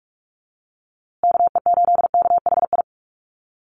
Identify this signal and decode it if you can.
Morse code, heard as a keyed tone, about 44 wpm, 700 Hz. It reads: KE8KHI
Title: USA Call Signs Repeated 2x 45wpm